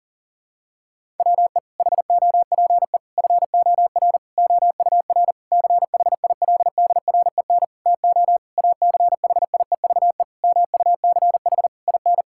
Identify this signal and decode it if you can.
Morse code, heard as a keyed tone, about 37 wpm, 700 Hz. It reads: WE HOPE FOR OUR CHILDREN TO ACHIEVE MUCH IN
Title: Sentences from Top 1000 Words - Review Part 6 - 40wpm